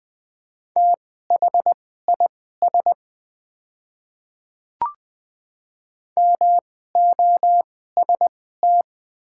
Morse code, heard as a keyed tone, about 20 wpm, spaced as for 14 wpm, 700 Hz. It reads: THIS MOST
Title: 2x Character Spacing - Top 100 Words - 20wpm